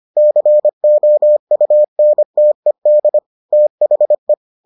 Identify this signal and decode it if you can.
Morse code, heard as a keyed tone, about 25 wpm, 600 Hz. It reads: COUNTED THE